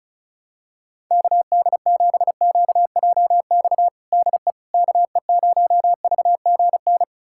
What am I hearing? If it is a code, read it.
Morse code, heard as a keyed tone, about 34 wpm, 700 Hz. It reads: KD7QJX DE KE0VGN
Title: USA Callsign DE Callsign 35wpm